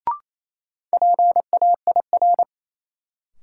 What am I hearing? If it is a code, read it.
Morse code, heard as a keyed tone, about 27 wpm, 700 Hz. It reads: PAIR
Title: Sets of 3 Words - New Word Drill - 28wpm